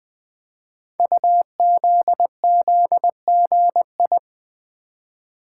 Morse code, heard as a keyed tone, about 20 wpm, 700 Hz. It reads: UZZGI
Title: Code Groups - 5 Letters - 20wpm